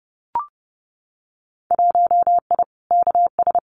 Morse code, heard as a keyed tone, about 30 wpm, 700 Hz. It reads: 1I KS
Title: Winter Field Day Exchanges Encore 30wpm